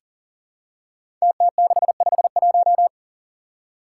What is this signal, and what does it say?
Morse code, heard as a keyed tone, about 38 wpm, 700 Hz. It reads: TT651
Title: Five Challenging Letter-Number Combinations 40wpm